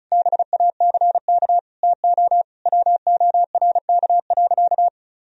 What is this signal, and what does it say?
Morse code, heard as a keyed tone, about 35 wpm, 700 Hz. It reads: BACK TO WORK.
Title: Sentences from Top 1000 Words - Review Part 1 - 35wpm